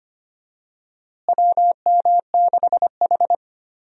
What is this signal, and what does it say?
Morse code, heard as a keyed tone, about 25 wpm, 700 Hz. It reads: WM6H
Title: USA Call Signs - 4 Characters - Repeated 2x - 25wpm